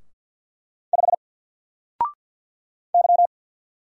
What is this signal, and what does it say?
Morse code, heard as a keyed tone, about 50 wpm, 700 Hz. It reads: 5 Y
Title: Single Letter-Number 50wpm